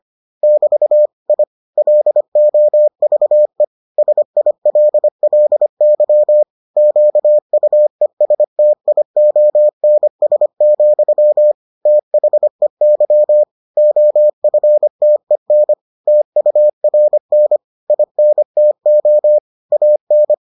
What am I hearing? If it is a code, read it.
Morse code, heard as a keyed tone, about 25 wpm, 600 Hz. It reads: <BT> I LOVE SILLY QUESTIONS, THEY OFTEN TURN INTO AN